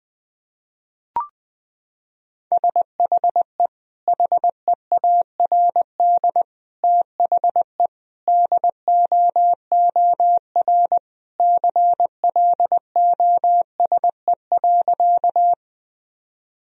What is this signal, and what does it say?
Morse code, heard as a keyed tone, about 20 wpm, 700 Hz. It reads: SHE HEARD THE DOOR CLOSE.
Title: Sentences from Top 1000 Words - Review Part 1 - 20wpm